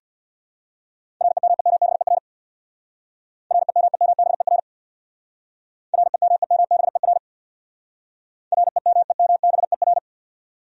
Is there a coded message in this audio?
Yes: Morse code from 1.2 to 10.0 s, heard as a keyed tone, about 67 wpm, 700 Hz. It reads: REMEMBER REMEMBER REMEMBER REMEMBER